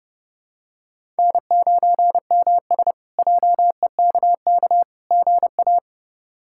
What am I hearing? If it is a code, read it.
Morse code, heard as a keyed tone, about 30 wpm, 700 Hz. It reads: N9MS JEKK GA
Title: K1USN SST QSO Practice - 30wpm